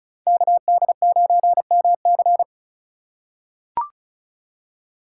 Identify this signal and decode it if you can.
Morse code, heard as a keyed tone, about 34 wpm, 700 Hz. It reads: KD9MC
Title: USA Call Signs - 5 Characters - 4x Character Spacing - 35wpm